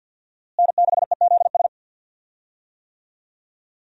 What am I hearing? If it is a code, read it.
Morse code, heard as a keyed tone, about 52 wpm, 700 Hz. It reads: N6EZS
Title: USA Call Signs - 5 Characters - Repeated 2x - 50wpm